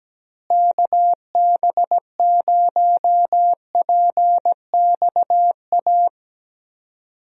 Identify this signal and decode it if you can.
Morse code, heard as a keyed tone, about 17 wpm, 700 Hz. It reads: KB0PXA
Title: USA Call Signs - 6 Characters - Repeated 3x - 17wpm